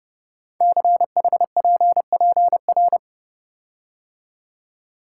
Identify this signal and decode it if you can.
Morse code, heard as a keyed tone, about 28 wpm, 700 Hz. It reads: CHPPR
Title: Code Groups - 5 Letters - 30wpm